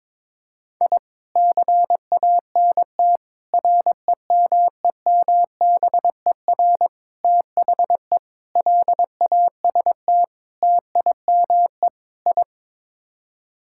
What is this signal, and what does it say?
Morse code, heard as a keyed tone, about 21 wpm, 700 Hz. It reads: I CANT REMEMBER THE LAST TIME I
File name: practice / 22wpm / Sets of 7 Words - 22wpm